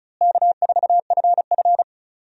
Morse code, heard as a keyed tone, about 36 wpm, 700 Hz. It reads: K4FF
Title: USA Call Signs - 4 Characters - Repeated 3x - 35wpm